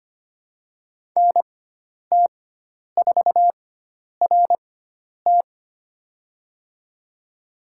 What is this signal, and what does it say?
Morse code, heard as a keyed tone, about 25 wpm, spaced as for 10 wpm, 700 Hz. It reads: NT4RT
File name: Lesson 16g - Letter c - Call Signs 25wpm